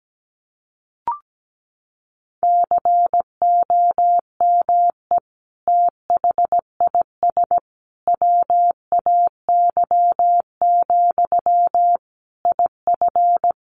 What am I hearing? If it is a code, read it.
Morse code, heard as a keyed tone, about 17 wpm, 700 Hz. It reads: COME THIS WAY, IF
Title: Sentences from Top 1500 Words 17wpm